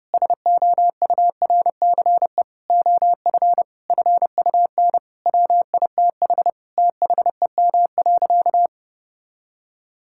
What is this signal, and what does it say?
Morse code, heard as a keyed tone, about 30 wpm, 700 Hz. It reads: SOURCE OF FUN WITH THEM.
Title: Sentences from Top 1000 Words - Review Part 5 - 30wpm